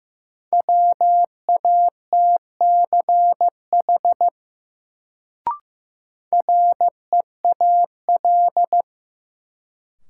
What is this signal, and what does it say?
Morse code, heard as a keyed tone, about 15 wpm, 700 Hz. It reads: WATCH REAL
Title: Sentences from Top 300 Words - New Word Drill - 15wpm